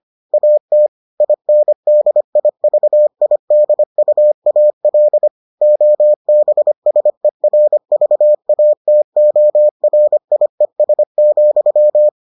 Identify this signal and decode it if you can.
Morse code, heard as a keyed tone, about 25 wpm, 600 Hz. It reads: AT INDIVIDUAL OBSERVATORIES,